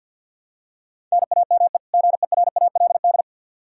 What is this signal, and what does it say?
Morse code, heard as a keyed tone, about 52 wpm, 700 Hz. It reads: NAME GERALD
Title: Practice QSO - Exchange 1 - Beginning - 50wpm